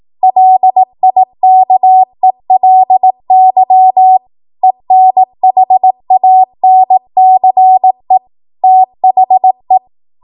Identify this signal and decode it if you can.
Morse code, heard as a keyed tone, about 18 wpm, 750 Hz. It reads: LIKELY ENHANCE THE
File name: Morse / 18WPM